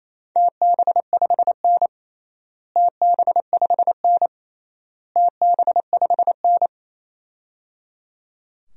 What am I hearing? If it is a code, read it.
Morse code, heard as a keyed tone, about 28 wpm, 700 Hz. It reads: TB5N TB5N TB5N
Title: DXCC-100 Call Signs - 4 Characters - Repeated 3x - 28wpm